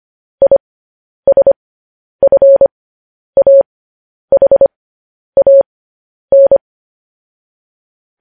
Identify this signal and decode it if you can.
Morse code, heard as a keyed tone, about 25 wpm, spaced as for 10 wpm, 550 Hz. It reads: ISFAHAN